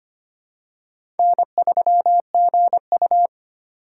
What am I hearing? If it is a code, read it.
Morse code, heard as a keyed tone, about 25 wpm, 700 Hz. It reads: N3GU
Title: USA Call Signs - 4 Characters - 4x Character Spacing - 25wpm